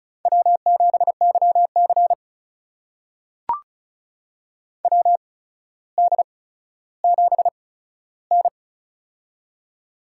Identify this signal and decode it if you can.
Morse code, heard as a keyed tone, about 36 wpm, spaced as for 15 wpm, 700 Hz. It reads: W7YC WD7N
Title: USA Call Signs - 4 Characters - 8x Character Spacing - 35wpm